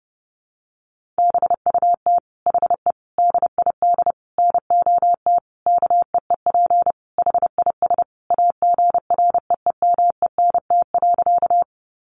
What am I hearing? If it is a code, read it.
Morse code, heard as a keyed tone, about 30 wpm, 700 Hz. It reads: BUT HE DID NOT KEEP HIS AGREEMENT.
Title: Sentences from Top 1500 Words 30wpm